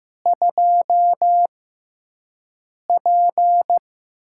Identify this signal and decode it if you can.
Morse code, heard as a keyed tone, about 15 wpm, spaced as for 5 wpm, 700 Hz. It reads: 2P
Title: USA Call Signs - 6x Character Spacing - 15wpm